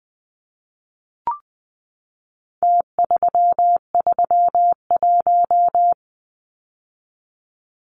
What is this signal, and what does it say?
Morse code, heard as a keyed tone, about 20 wpm, 700 Hz. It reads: T331
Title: Numbers - Contest Style - 4-Digit 20wpm